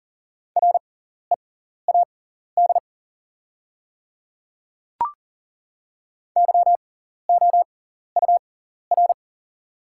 Morse code, heard as a keyed tone, about 39 wpm, spaced as for 14 wpm, 690 Hz. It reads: READ YOUR
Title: 6x Character Spacing - Top 200 Words - 40wpm